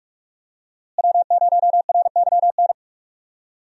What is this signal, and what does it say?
Morse code, heard as a keyed tone, about 46 wpm, 700 Hz. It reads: W0RYN
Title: USA Call Signs - 5 Characters - Repeated 3x - 45wpm